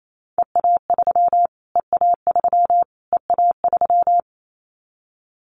Morse code, heard as a keyed tone, about 28 wpm, 700 Hz. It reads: EA3 EA3 EA3
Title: Callsign Prefixes for Top 25 DXCC - Repeated 3x - 28wpm